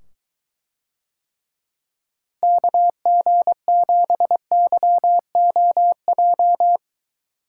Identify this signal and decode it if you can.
Morse code, heard as a keyed tone, about 23 wpm, 700 Hz. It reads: KG7YOJ